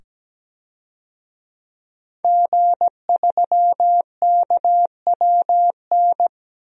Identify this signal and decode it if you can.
Morse code, heard as a keyed tone, about 17 wpm, 700 Hz. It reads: G3KWN
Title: DXCC-100 Call Signs - 5 Characters - 17wpm